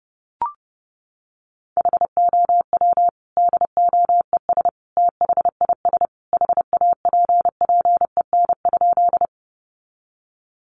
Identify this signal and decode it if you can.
Morse code, heard as a keyed tone, about 30 wpm, 700 Hz. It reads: HOW DOES THIS HAPPEN?
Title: Sentences from Top 400 Words 30wpm